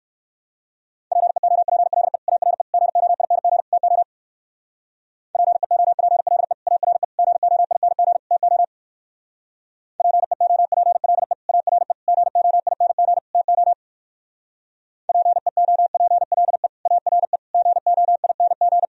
Morse code, heard as a keyed tone, about 58 wpm, 700 Hz. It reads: PEOPLE ARE GOING TO PEOPLE ARE GOING TO PEOPLE ARE GOING TO PEOPLE ARE GOING